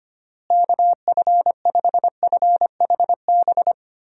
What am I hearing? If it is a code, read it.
Morse code, heard as a keyed tone, about 25 wpm, 700 Hz. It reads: KF5FHB
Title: USA Call Signs - 6 Characters - Repeated 3x - 25wpm